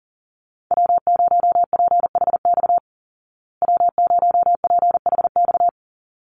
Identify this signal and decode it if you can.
Morse code, heard as a keyed tone, about 40 wpm, 700 Hz. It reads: W0PHX W0PHX
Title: Call Signs Repeated 3x - 40wpm